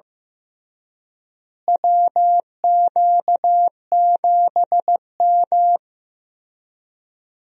Morse code, heard as a keyed tone, about 16 wpm, 700 Hz. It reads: WQ7M
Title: USA Call Signs - 4 Characters - Repeated 3x - 15wpm